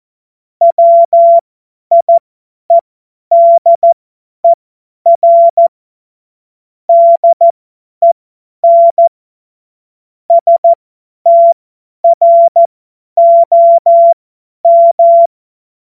Morse code, heard as a keyed tone, about 14 wpm, spaced as for 10 wpm, 680 Hz. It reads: WIEDER DEN STROM
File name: Morsekurs / Witze / Witz 07. Februar 2023 50 F70